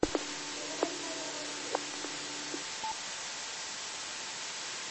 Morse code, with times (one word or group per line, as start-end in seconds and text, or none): none